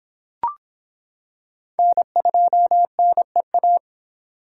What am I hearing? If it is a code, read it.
Morse code, heard as a keyed tone, about 26 wpm, 700 Hz. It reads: N2NEA